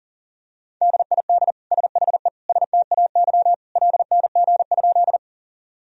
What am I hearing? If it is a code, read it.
Morse code, heard as a keyed tone, about 40 wpm, 700 Hz. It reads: DID SHE STAY LNG?
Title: Sentences from Top 500 Words and Abbreviations 40wpm